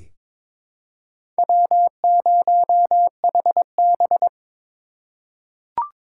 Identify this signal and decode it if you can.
Morse code, heard as a keyed tone, about 22 wpm, 700 Hz. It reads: W0HB
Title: USA Call Signs - 4 Characters - 4x Character Spacing - 22wpm